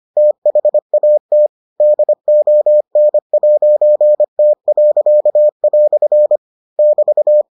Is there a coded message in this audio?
Yes: Morse code from 0.2 to 7.4 s, heard as a keyed tone, about 25 wpm, 600 Hz. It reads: THAT DON'T." <BT>